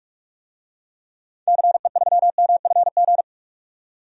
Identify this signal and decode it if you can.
Morse code, heard as a keyed tone, about 46 wpm, 700 Hz. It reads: CE3MFG
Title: DXCC-100 Call Signs - 6 Characters - 45wpm